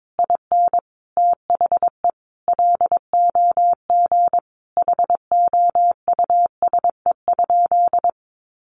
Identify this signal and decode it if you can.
Morse code, heard as a keyed tone, about 22 wpm, 700 Hz. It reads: IN THE LOG HOUSE?
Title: Sentences from Top 1400 Words 22wpm